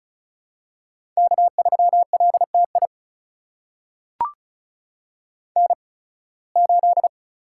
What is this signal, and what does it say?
Morse code, heard as a keyed tone, about 34 wpm, spaced as for 23 wpm, 700 Hz. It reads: K3LTI N8
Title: USA Call Signs - 5 Characters - 8x Character Spacing - 35wpm